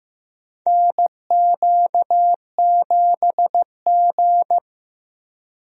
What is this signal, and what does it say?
Morse code, heard as a keyed tone, about 15 wpm, 700 Hz. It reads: NQ7G
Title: USA Call Signs - 4 Characters - Repeated 3x - 15wpm